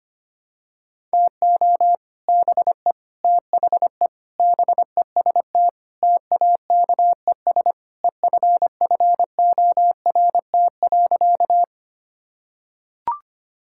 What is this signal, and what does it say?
Morse code, heard as a keyed tone, about 25 wpm, 700 Hz. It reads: TO BE THE BEST TAKES EFFORT.
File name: practice / 25wpm / Sentences with 3-Word Phrases - 25wpm